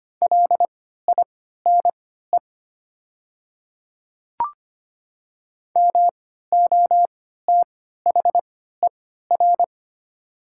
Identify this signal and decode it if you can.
Morse code, heard as a keyed tone, about 25 wpm, spaced as for 14 wpm, 700 Hz. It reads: LINE MOTHER
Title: 3x Character Spacing - Top 200 Words - 25wpm